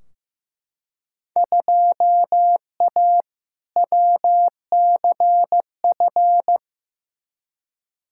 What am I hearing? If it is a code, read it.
Morse code, heard as a keyed tone, about 15 wpm, 700 Hz. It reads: 2A WCF
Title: Field Day Exchanges - 15wpm